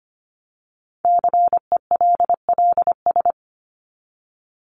Morse code, heard as a keyed tone, about 25 wpm, 700 Hz. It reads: CELLS
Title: ES:S Suffix - Top 100 Words with Base Words 25wpm